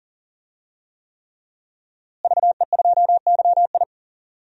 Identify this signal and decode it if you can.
Morse code, heard as a keyed tone, about 40 wpm, 700 Hz. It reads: VE2YI
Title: DXCC-100 Call Signs - 5 Characters - Repeated 3x - 40wpm